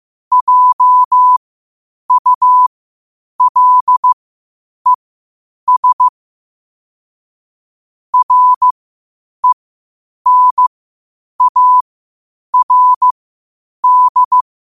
Morse code, heard as a keyed tone, about 15 wpm, spaced as for 8 wpm, 1000 Hz. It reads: JULES RENARD
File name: qotdfiles / 8wpm